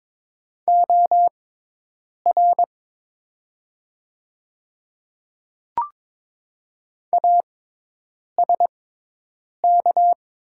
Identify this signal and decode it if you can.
Morse code, heard as a keyed tone, about 22 wpm, spaced as for 8 wpm, 700 Hz. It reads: OR ASK
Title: 6x Character Spacing - Top 200 Words - 22wpm